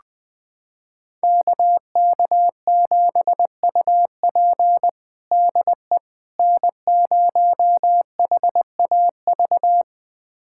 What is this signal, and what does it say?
Morse code, heard as a keyed tone, about 20 wpm, 700 Hz. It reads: KK7UP DE N0HAV
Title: USA Callsign DE Callsign - 5 Characters - 20wpm